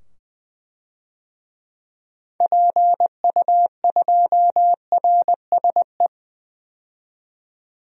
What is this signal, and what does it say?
Morse code, heard as a keyed tone, about 20 wpm, 700 Hz. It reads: PU2RSE
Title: DXCC-100 Call Signs - 6 Characters - 20wpm